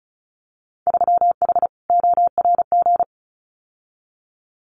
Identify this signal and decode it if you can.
Morse code, heard as a keyed tone, about 34 wpm, 700 Hz. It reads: 3H ORG
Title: Winter Field Day Exchanges Encore 35wpm